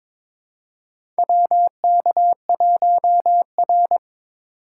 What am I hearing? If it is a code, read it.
Morse code, heard as a keyed tone, about 22 wpm, 700 Hz. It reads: WK1R
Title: USA Call Signs - 4 Characters - 3x Character Spacing - 22wpm